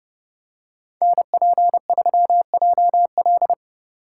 Morse code, heard as a keyed tone, about 30 wpm, 700 Hz. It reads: NP3JL